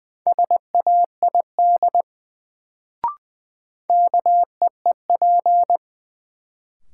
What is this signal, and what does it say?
Morse code, heard as a keyed tone, about 21 wpm, 700 Hz. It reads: SAID KEEP
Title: Sets of 2 Words - New Word Drill - 20wpm